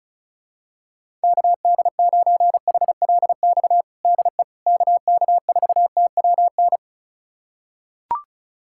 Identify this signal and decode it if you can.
Morse code, heard as a keyed tone, about 34 wpm, 700 Hz. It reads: KD9HLX DE KK4TWN
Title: USA Callsign DE Callsign - 6 Characters - 35wpm